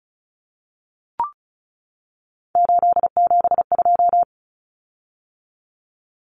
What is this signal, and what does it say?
Morse code, heard as a keyed tone, about 34 wpm, 700 Hz. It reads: 872
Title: Common Cut Numbers - 3-Digit 35wpm